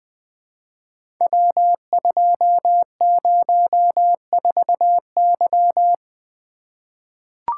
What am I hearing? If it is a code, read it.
Morse code, heard as a keyed tone, about 20 wpm, 700 Hz. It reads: W204Y
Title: Code Groups - 5 Letter:Numbers - 20wpm